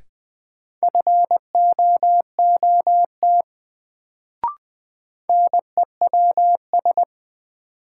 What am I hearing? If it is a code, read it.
Morse code, heard as a keyed tone, about 20 wpm, 700 Hz. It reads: FOOT NEWS
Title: Sets of 5 Words - New Word Drill - 20wpm